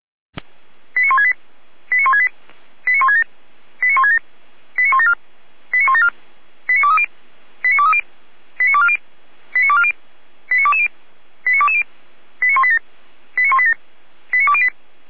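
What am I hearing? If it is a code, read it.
Five-tone selective call (ZVEI), heard as quick runs of five tones, tones of about 70 ms. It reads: 89168 89168 89167 89167 89174 89174 89230 89230 89240 89240 89220 89220 89187 89187 89208
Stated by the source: ZVEI five-tone call 89163 received by SDR.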